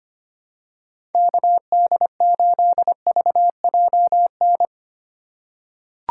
Morse code, heard as a keyed tone, about 25 wpm, 700 Hz. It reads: KD8VJN
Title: USA Call Signs - 3x Character Spacing - 25wpm